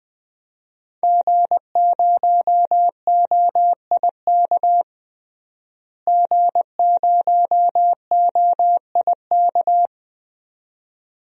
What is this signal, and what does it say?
Morse code, heard as a keyed tone, about 20 wpm, 700 Hz. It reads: G0OIK G0OIK
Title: DXCC-100 Call Signs - 5 Characters - Repeated 2x - 20wpm